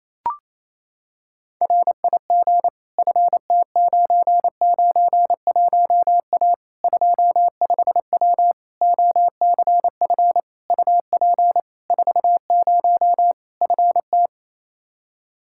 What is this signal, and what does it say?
Morse code, heard as a keyed tone, about 28 wpm, 700 Hz. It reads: RIG FT991A 25W OCF UP 40 FT
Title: Practice QSO - Exchange 2 - Info - 28wpm